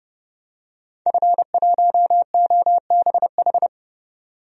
Morse code, heard as a keyed tone, about 32 wpm, 700 Hz. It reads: F1OBH
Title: DXCC-100 Call Signs - 5 Characters - 30wpm